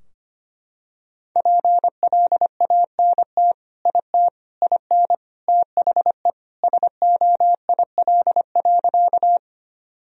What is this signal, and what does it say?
Morse code, heard as a keyed tone, about 25 wpm, 700 Hz. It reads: PLANT IT IN THE SOIL.
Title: Sentences from Top 1000 Words - Review Part 4 - 25wpm